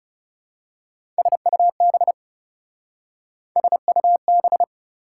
Morse code, heard as a keyed tone, about 34 wpm, 700 Hz. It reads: SUB SUB
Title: Speed-Racing Prefixes and Suffixes 30wpm